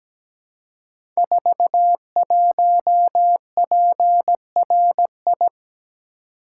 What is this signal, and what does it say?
Morse code, heard as a keyed tone, about 17 wpm, 700 Hz. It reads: 41PRI
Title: Code Groups - 5 Letter:Numbers - 17wpm